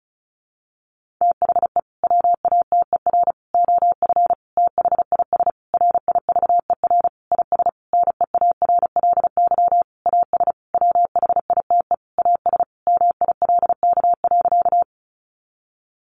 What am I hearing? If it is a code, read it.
Morse code, heard as a keyed tone, about 34 wpm, 700 Hz. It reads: THE WATER OF THIS RIVER IS NEARLY AS WHITE AS MILK.
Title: Sentences from Top 1500 Words 35wpm